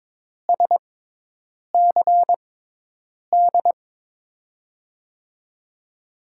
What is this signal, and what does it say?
Morse code, heard as a keyed tone, about 22 wpm, spaced as for 8 wpm, 700 Hz. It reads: SCD
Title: USA Call Signs - 6 Characters - 6x Character Spacing - 22wpm